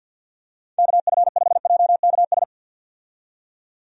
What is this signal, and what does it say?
Morse code, heard as a keyed tone, about 51 wpm, 700 Hz. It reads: KF5JKS
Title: USA Call Signs - 6 Characters - 50wpm